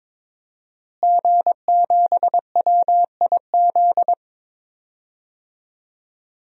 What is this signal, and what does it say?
Morse code, heard as a keyed tone, about 22 wpm, 700 Hz. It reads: G7WIZ
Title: DXCC-100 Call Signs - 5 Characters - Repeated 3x - 22wpm